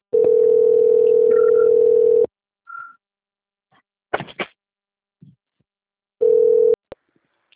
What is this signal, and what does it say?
Ringback tone (North America), 440 + 480 Hz.